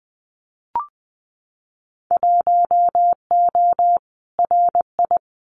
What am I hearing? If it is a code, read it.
Morse code, heard as a keyed tone, about 20 wpm, 700 Hz. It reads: 1O RI
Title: Winter Field Day Exchanges 20wpm